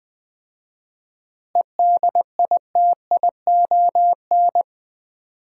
Morse code, heard as a keyed tone, about 21 wpm, 700 Hz. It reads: EDITION
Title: 4x Word-Spacing - Sets of 5 Words - 20wpm